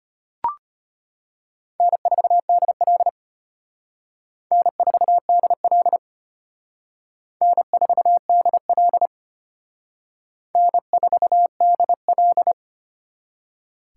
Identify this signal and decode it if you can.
Morse code, heard as a keyed tone, about 33 wpm, 700 Hz. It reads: N4DL N4DL N4DL N4DL